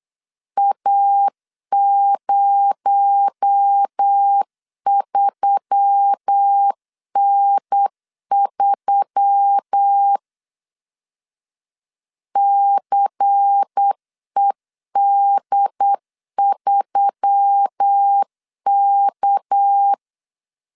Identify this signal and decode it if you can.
Morse code, heard as a keyed tone, about 8 wpm, 800 Hz. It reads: A03N3 CED3K
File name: CW / DET / 1 6 1500C S C 8wpm 5219